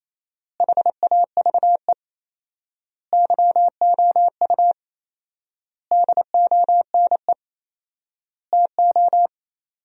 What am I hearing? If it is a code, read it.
Morse code, heard as a keyed tone, about 28 wpm, 700 Hz. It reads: HAVE YOU DONE TO
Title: 4x Word-Spacing - Sentences from Top 500 Words - 28wpm